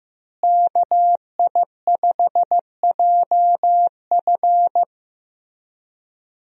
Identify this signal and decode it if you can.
Morse code, heard as a keyed tone, about 15 wpm, 700 Hz. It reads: KI5JF
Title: USA Call Signs - 5 Characters - Repeated 3x - 15wpm